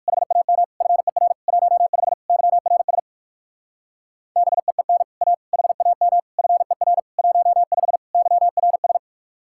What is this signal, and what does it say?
Morse code, heard as a keyed tone, about 56 wpm, 700 Hz. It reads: HAM FER 15 YRS BEEN A HAM FER 15 YRS
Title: Speed-Racing Common QSO Phrases - 45wpm